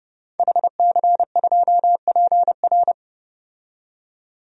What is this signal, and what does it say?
Morse code, heard as a keyed tone, about 30 wpm, 700 Hz. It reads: HC2PR
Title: DXCC-100 Call Signs - 5 Characters - 30wpm